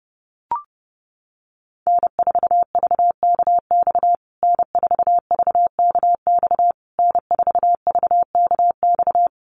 Morse code, heard as a keyed tone, about 30 wpm, 700 Hz. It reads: N4VKX N4VKX N4VKX
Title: USA Call Signs Repeated 3x 30wpm